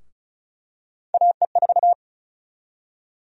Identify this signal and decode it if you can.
Morse code, heard as a keyed tone, about 36 wpm, 700 Hz. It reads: AE4
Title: USA Call Sign Prefixes - 35wpm